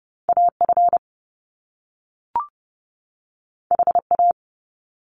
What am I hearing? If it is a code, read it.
Morse code, heard as a keyed tone, about 30 wpm, 700 Hz. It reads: AF HA
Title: Top Two Letter Combinations in Words 30wpm